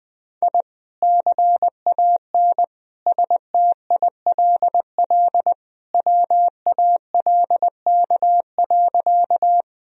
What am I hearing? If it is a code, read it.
Morse code, heard as a keyed tone, about 21 wpm, 700 Hz. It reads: I CAN STILL WALK.
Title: Sentences from Top 1000 Words - Review Part 1 - 20wpm